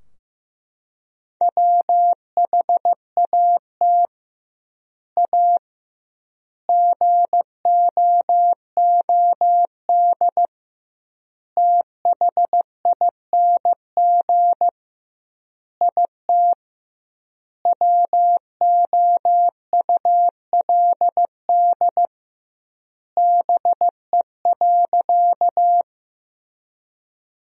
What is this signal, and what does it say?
Morse code, heard as a keyed tone, about 15 wpm, 700 Hz. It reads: WHAT A GOOD THING IT WOULD BE.
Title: 2x Word-Spacing - Sentences from Top 200 Words - 15wpm